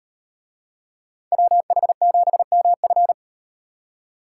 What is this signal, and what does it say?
Morse code, heard as a keyed tone, about 38 wpm, 700 Hz. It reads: WH7MF